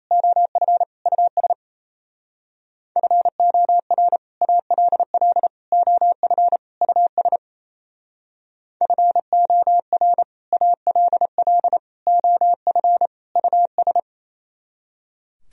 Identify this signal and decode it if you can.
Morse code, heard as a keyed tone, about 36 wpm, 700 Hz. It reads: OF US FOR ALL OF US FOR ALL OF US